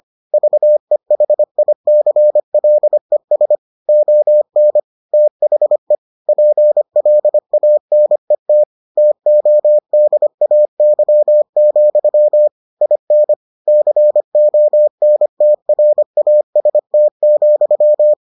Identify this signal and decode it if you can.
Morse code, heard as a keyed tone, about 25 wpm, 600 Hz. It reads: VEHICLES ON THE PLANET TODAY, IN CONTRAST,